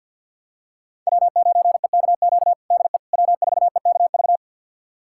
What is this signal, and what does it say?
Morse code, heard as a keyed tone, about 52 wpm, 700 Hz. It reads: W9EKQ DE W4EKV
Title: USA Callsign DE Callsign - 5 Characters - 50wpm